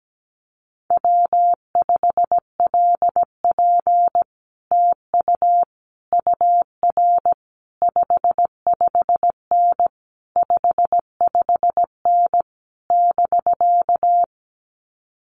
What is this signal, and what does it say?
Morse code, heard as a keyed tone, about 17 wpm, 700 Hz. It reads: W5LP TU UR 55N 55N <BK>
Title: Common QSO Phrases Encore 17wpm